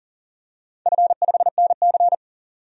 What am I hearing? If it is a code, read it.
Morse code, heard as a keyed tone, about 40 wpm, 700 Hz. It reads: F5NC
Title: DXCC-100 Call Signs - 4 Characters - 40wpm